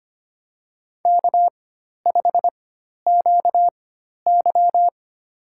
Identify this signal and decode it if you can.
Morse code, heard as a keyed tone, about 25 wpm, spaced as for 12 wpm, 700 Hz. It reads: K5QY
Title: USA Call Signs - 4x Character Spacing - 25wpm